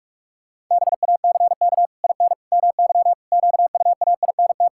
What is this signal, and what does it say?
Morse code, heard as a keyed tone, about 46 wpm, 700 Hz. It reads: BACK IN MY QUAINT
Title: Pangram Challenge v2 - 45wpm